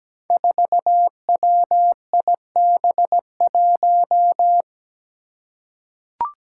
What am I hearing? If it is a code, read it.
Morse code, heard as a keyed tone, about 17 wpm, 700 Hz. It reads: 4WIB1
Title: Five Challenging Letter-Number Combinations 17wpm